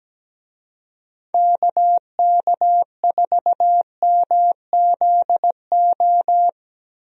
Morse code, heard as a keyed tone, about 17 wpm, 700 Hz. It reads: KK4MZO
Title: USA Call Signs - 6 Characters - 17wpm